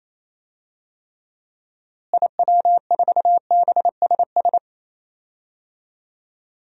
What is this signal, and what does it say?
Morse code, heard as a keyed tone, about 28 wpm, 700 Hz. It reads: IW4BSS
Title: DXCC-100 Call Signs - 6 Characters - Repeated 3x - 28wpm